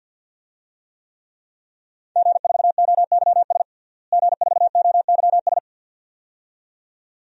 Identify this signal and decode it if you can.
Morse code, heard as a keyed tone, about 52 wpm, 700 Hz. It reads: G4OYS G4OYS
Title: DXCC-100 Call Signs - 5 Characters - Repeated 2x - 50wpm